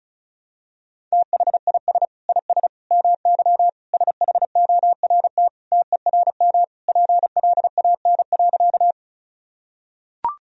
Morse code, heard as a keyed tone, about 36 wpm, 700 Hz. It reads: THIS IS MY SHORT TERM PLAN.
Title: Sentences from Top 1000 Words - Review Part 4 - 35wpm